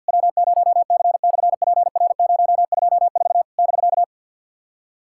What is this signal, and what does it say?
Morse code, heard as a keyed tone, about 50 wpm, 700 Hz. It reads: W0C/PR024 <BK>
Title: SOTA QSO Practice 50wpm